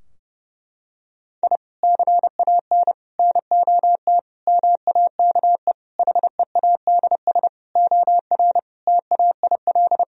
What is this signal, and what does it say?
Morse code, heard as a keyed tone, about 32 wpm, 700 Hz. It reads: I CAN NOT MAKE HEADS OR TAIL
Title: Sentences from Top 1000 Words - Review Part 3 - 30wpm